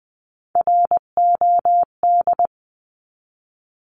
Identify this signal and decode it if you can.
Morse code, heard as a keyed tone, about 20 wpm, 700 Hz. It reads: ROD
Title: 259 Lesson 42d - Words 18-20wpm